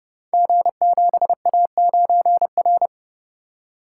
Morse code, heard as a keyed tone, about 30 wpm, 700 Hz. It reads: G7A9R
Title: Code Groups - 5 Letter:Numbers - 30wpm